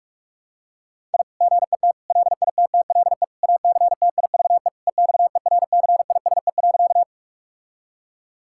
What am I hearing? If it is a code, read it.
Morse code, heard as a keyed tone, about 44 wpm, 700 Hz. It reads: I GET LITTLE ACTIVE EXERCISE.
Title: Sentences from Top 1000 Words - Review Part 6 - 45wpm